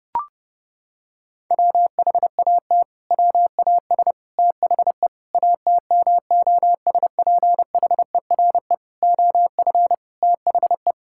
Morse code, heard as a keyed tone, about 30 wpm, 700 Hz. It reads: WHAT WAS THE ATMOSPHERE OF THE
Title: Sentences from Top 1600 Words 30wpm